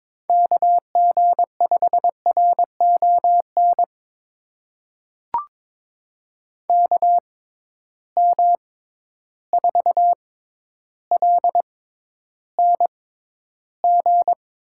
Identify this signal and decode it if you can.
Morse code, heard as a keyed tone, about 22 wpm, spaced as for 11 wpm, 700 Hz. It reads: KG5RON KM4LNG
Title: USA Call Signs - 6 Characters - 6x Character Spacing - 22wpm